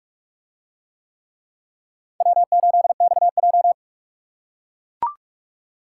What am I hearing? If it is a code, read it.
Morse code, heard as a keyed tone, about 41 wpm, 700 Hz. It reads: W8XJ